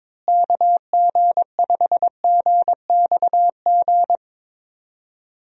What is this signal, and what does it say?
Morse code, heard as a keyed tone, about 22 wpm, 700 Hz. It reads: KG5GXG
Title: USA Call Signs - 6 Characters - Repeated 2x - 22wpm